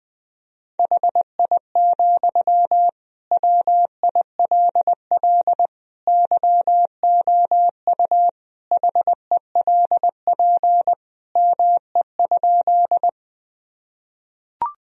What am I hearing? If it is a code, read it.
Morse code, heard as a keyed tone, about 20 wpm, 700 Hz. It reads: HI, WILL YOU HELP ME?
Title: Sentences from Top 1000 Words - Review Part 3 - 20wpm